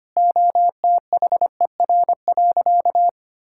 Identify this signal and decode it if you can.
Morse code, heard as a keyed tone, about 25 wpm, 700 Hz. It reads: OTHER.
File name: practice / 25wpm / Sentences with 2-Word Phrases - 25wpm